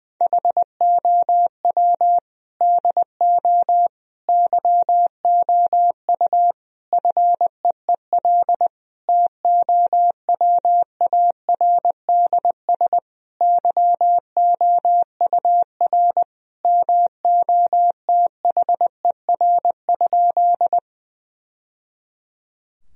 Morse code, heard as a keyed tone, about 21 wpm, 700 Hz. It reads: HOW DO YOU FEEL TOWARDS YOUR MOTHER?
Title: Sentences from Top 1000 Words - Review Part 2 - 20wpm